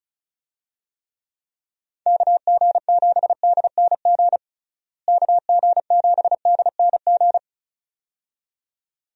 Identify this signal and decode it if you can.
Morse code, heard as a keyed tone, about 34 wpm, 700 Hz. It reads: KG7DNG KG7DNG
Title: USA Call Signs - 6 Characters - Repeated 2x - 35wpm